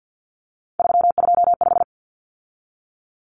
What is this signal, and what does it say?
Morse code, heard as a keyed tone, about 49 wpm, 700 Hz. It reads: 325
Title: Numbers - 3-Digit 50wpm